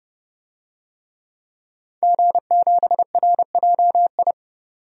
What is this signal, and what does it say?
Morse code, heard as a keyed tone, about 31 wpm, 700 Hz. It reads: G7RJI
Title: DXCC-100 Call Signs - 5 Characters - 30wpm